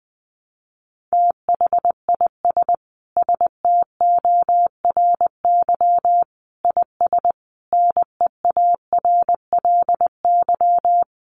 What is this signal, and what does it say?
Morse code, heard as a keyed tone, about 20 wpm, 700 Hz. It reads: THIS STORY IS NEARLY
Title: Sentences from Top 1300 Words 20wpm